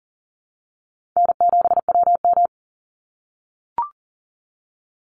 Morse code, heard as a keyed tone, about 40 wpm, 700 Hz. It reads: N7WM
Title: NA SOTA Chaser Callsigns 40wpm